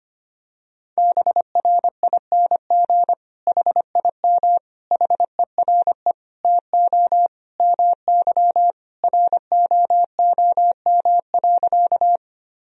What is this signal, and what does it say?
Morse code, heard as a keyed tone, about 25 wpm, 700 Hz. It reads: BRING HIM HERE TO MY ROOM.
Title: Sentences from Top 1000 Words - Review Part 3 - 25wpm